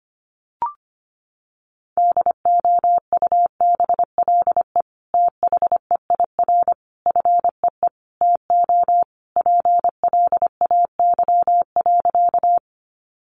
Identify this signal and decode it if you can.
Morse code, heard as a keyed tone, about 25 wpm, 700 Hz. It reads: DOUBLE THEIR FEE TO PLAY.
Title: Sentences from Top 1500 Words 25wpm